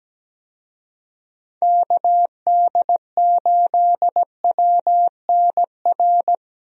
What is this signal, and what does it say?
Morse code, heard as a keyed tone, about 17 wpm, 700 Hz. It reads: KD8WNR
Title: USA Call Signs - 6 Characters - 17wpm